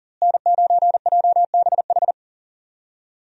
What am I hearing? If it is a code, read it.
Morse code, heard as a keyed tone, about 37 wpm, 700 Hz. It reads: N9JBH
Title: USA Call Signs - 5 Characters - Repeated 2x - 40wpm